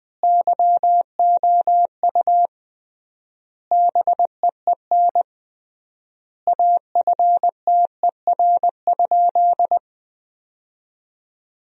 Morse code, heard as a keyed tone, about 20 wpm, 700 Hz. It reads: YOU BEEN AFTER?
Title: 3x Word-Spacing - Sentences from Top 200 Words - 20wpm